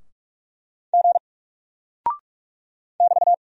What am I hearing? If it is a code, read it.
Morse code, heard as a keyed tone, about 45 wpm, 700 Hz. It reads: G <BT>
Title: Single Character (Letter:Number:Punctuation:Prosign) Alt 45wpm